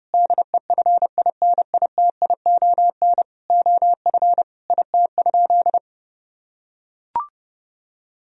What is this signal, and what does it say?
Morse code, heard as a keyed tone, about 30 wpm, 700 Hz. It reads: DEFINITION OF IT?
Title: Sentences from Top 1000 Words - Review Part 4 - 30wpm